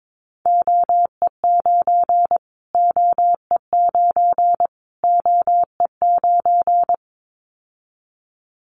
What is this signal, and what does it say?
Morse code, heard as a keyed tone, about 22 wpm, 700 Hz. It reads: OE9 OE9 OE9
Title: Callsign Prefixes for Top 25 DXCC - Repeated 3x - 22wpm